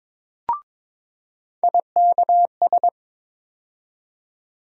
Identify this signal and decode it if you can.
Morse code, heard as a keyed tone, about 21 wpm, 700 Hz. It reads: IKS
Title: USA Call Sign Postfixes - 3 Letters 22wpm